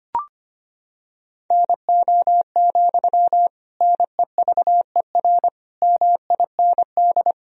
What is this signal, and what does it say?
Morse code, heard as a keyed tone, about 25 wpm, 700 Hz. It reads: NO, NEVER MIND
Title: Sentences from Top 1000 Words - Review Part 1 - 25wpm